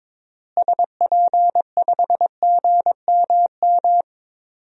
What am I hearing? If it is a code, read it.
Morse code, heard as a keyed tone, about 22 wpm, 700 Hz. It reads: SP5GMM
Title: DXCC-100 Call Signs - 6 Characters - Repeated 2x - 22wpm